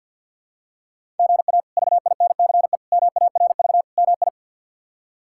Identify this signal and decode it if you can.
Morse code, heard as a keyed tone, about 52 wpm, 700 Hz. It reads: GA VINCE MARV MI
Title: K1USN SST QSO Practice - 50wpm